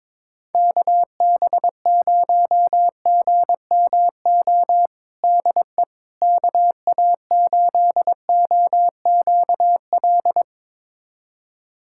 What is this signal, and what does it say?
Morse code, heard as a keyed tone, about 22 wpm, 700 Hz. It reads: KB0GMO DE KA8OQL
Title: USA Callsign DE Callsign 22wpm